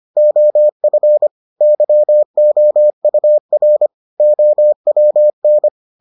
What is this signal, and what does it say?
Morse code, heard as a keyed tone, about 25 wpm, 600 Hz. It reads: OF YOUR OWN